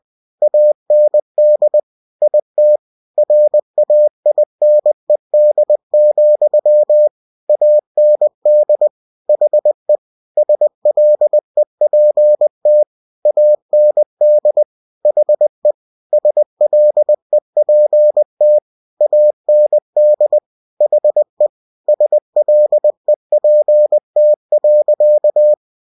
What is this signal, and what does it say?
Morse code, heard as a keyed tone, about 20 wpm, 600 Hz. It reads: AND IT RAINED, AND HE SLEPT AND HE SLEPT AND HE SLEPT.
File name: cw-books / winnie-the-pooh / winnie-20wpm-page-0093